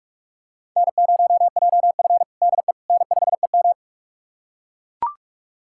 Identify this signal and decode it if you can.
Morse code, heard as a keyed tone, about 42 wpm, 700 Hz. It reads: N0JF DE N5EM
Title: USA Callsign DE Callsign - 4 Characters - 45wpm